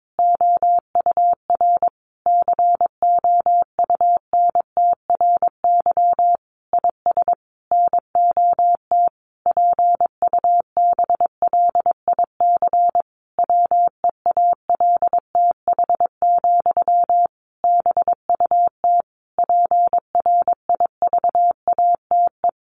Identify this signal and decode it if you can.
Morse code, heard as a keyed tone, about 22 wpm, 700 Hz. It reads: OUR COUNTRY IS NOT PUBLIC WEALTH, BUT PRIVATE
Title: Sentences from Top 1200 Words 22wpm